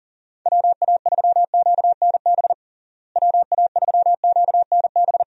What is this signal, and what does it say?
Morse code, heard as a keyed tone, about 40 wpm, 700 Hz. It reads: WA3QNB WA3QNB
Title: USA Call Signs - 6 Characters - Repeated 2x - 40wpm